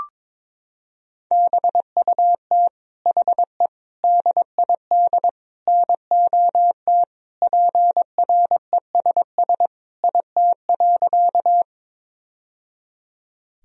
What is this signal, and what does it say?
Morse code, heard as a keyed tone, about 22 wpm, 700 Hz. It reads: BUT HE DID NOT PRESS IT.
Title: Sentences from Top 1000 Words - Review Part 1 - 22wpm